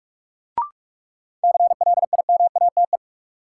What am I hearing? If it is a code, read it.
Morse code, heard as a keyed tone, about 45 wpm, 700 Hz. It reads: CLIMATE
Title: Sentences from Top 1600 Words - New Word Drill - 45wpm